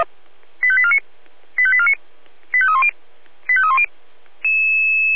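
Five-tone selective call (ZVEI), heard as quick runs of five tones, tones of about 70 ms. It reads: 85849 85849 85310 85310
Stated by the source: ZVEI five-tone call 85875 received by SDR.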